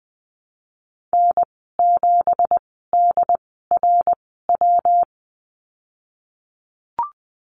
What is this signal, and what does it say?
Morse code, heard as a keyed tone, about 20 wpm, spaced as for 15 wpm, 700 Hz. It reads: N7DRW
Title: 167 Lesson 27g - Number 7 - Call Signs 20wpm